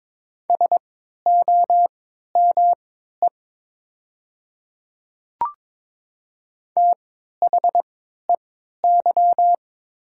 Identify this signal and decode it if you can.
Morse code, heard as a keyed tone, about 22 wpm, spaced as for 12 wpm, 700 Hz. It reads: SOME THEY
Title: 3x Character Spacing - Top 100 Words - 22wpm